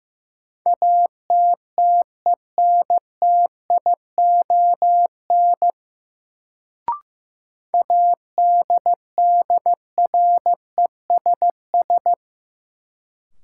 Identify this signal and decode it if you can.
Morse code, heard as a keyed tone, about 15 wpm, 700 Hz. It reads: ATTENTION ADDRESS
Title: Sentences from Top 1000 Words - New Word Drill - 15wpm